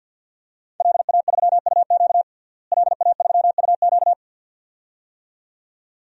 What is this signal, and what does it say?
Morse code, heard as a keyed tone, about 51 wpm, 700 Hz. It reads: LA3UQ LA3UQ
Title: DXCC-100 Call Signs - 5 Characters - Repeated 2x - 50wpm